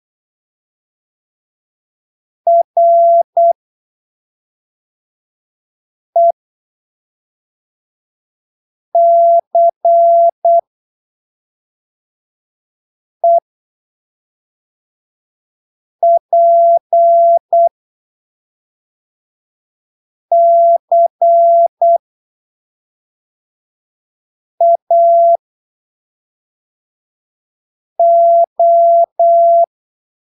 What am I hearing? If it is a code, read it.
Morse code, heard as a keyed tone, about 8 wpm, spaced as for 3 wpm, 680 Hz. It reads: RECEPCAO
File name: arquivos / texto / texto 01 08ppm